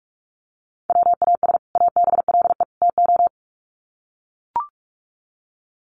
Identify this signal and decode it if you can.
Morse code, heard as a keyed tone, about 44 wpm, 700 Hz. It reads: WAS ABLE TO
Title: Sets of 3 Words 45wpm